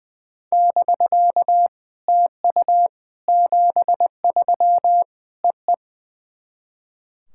Morse code, heard as a keyed tone, about 21 wpm, 700 Hz. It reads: <BK> TU 73 EE
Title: SOTA QSO Practice 20wpm